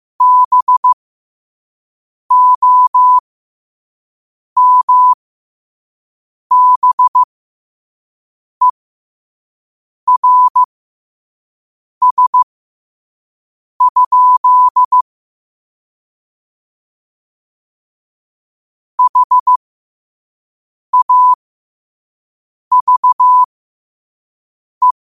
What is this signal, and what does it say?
Morse code, heard as a keyed tone, about 15 wpm, spaced as for 5 wpm, 1000 Hz. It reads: BOMBERS? HAVE
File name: qotdfiles / 5wpm